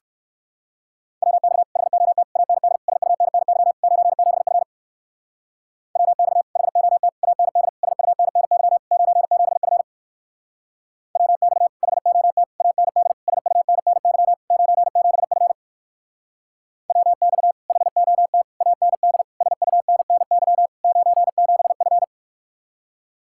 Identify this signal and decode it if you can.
Morse code, heard as a keyed tone, about 56 wpm, 700 Hz. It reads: WX HOT AND SUNNY 97F WX HOT AND SUNNY 97F WX HOT AND SUNNY 97F WX HOT AND SUNNY 97F
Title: Speed-Racing Common QSO Phrases Encore - 45wpm